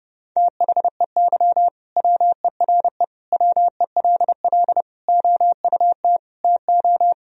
Morse code, heard as a keyed tone, about 32 wpm, 700 Hz. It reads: THEY WERE WELL OUT TO
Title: Sentences from Top 1000 Words - Review Part 3 - 30wpm